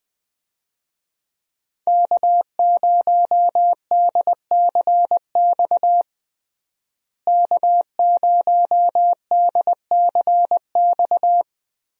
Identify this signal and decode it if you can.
Morse code, heard as a keyed tone, about 20 wpm, 700 Hz. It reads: K0DCX K0DCX
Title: USA Call Signs - 5 Characters - Repeated 2x - 20wpm